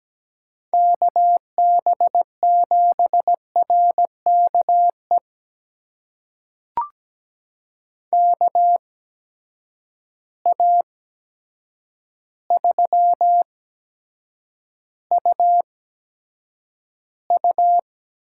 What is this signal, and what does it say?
Morse code, heard as a keyed tone, about 17 wpm, spaced as for 8 wpm, 700 Hz. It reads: KB7RKE KA3UU
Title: USA Call Signs - 6 Characters - 8x Character Spacing - 17wpm